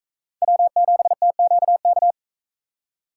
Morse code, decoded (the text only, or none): W7TQK